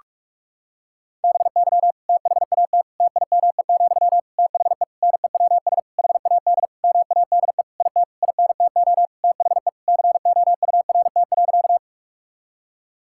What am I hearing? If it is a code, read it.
Morse code, heard as a keyed tone, about 41 wpm, 700 Hz. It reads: BY THAT TIME, THE NEWS HAD MADE IT INTO THE COURT.
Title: Sentences from Top 1600 Words 45wpm